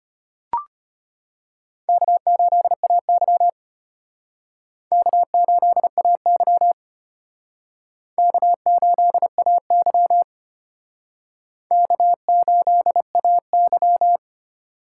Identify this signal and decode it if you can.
Morse code, heard as a keyed tone, about 31 wpm, 700 Hz. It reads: K8AY K8AY K8AY K8AY
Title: Speed Racing - USA 1x2 Call Signs 25wpm